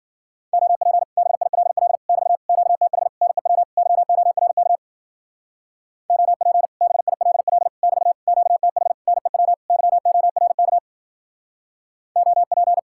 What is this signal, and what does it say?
Morse code, heard as a keyed tone, about 49 wpm, 700 Hz. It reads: OP BILL <BT> QTH NEW YORK OP BILL <BT> QTH NEW YORK OP
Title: Speed-Racing Common QSO Phrases - 40wpm